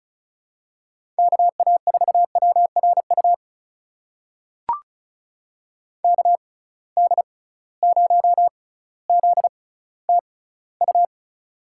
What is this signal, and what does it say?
Morse code, heard as a keyed tone, about 36 wpm, spaced as for 18 wpm, 700 Hz. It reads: KA4WRU KD0ZTU
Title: USA Call Signs - 6 Characters - 6x Character Spacing - 35wpm